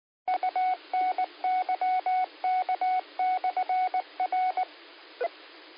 Morse code, heard as a keyed tone, about 19 wpm, 740 Hz. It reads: UNYK/R